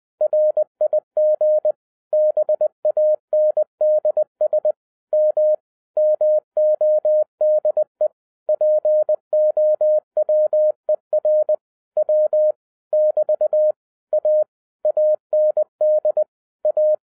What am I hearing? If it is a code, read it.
Morse code, heard as a keyed tone, about 20 wpm, 610 Hz. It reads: RIG BANDS M MODE POWER W <BT> A AND A
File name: 2011-07-mp3 / n23 QRP OTAKU 20wpm